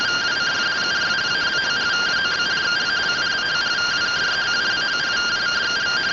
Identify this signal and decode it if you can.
Two-tone teleprinter signal, about 170 Hz shift.